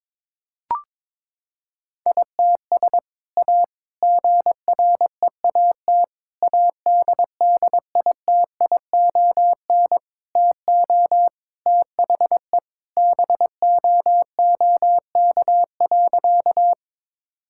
Morse code, heard as a keyed tone, about 22 wpm, 700 Hz. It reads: ITS A GREAT ADDITION TO THE BOOK.
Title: Sentences from Top 1000 Words - Review Part 6 - 22wpm